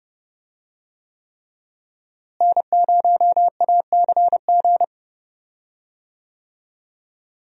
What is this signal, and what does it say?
Morse code, heard as a keyed tone, about 30 wpm, 700 Hz. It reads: N0ACG
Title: USA Call Signs - 5 Characters - Repeated 3x - 30wpm